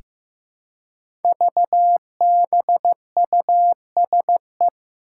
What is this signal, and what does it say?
Morse code, heard as a keyed tone, about 15 wpm, 700 Hz. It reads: VBUSE
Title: Five Challenging Letter-Number Combinations 15wpm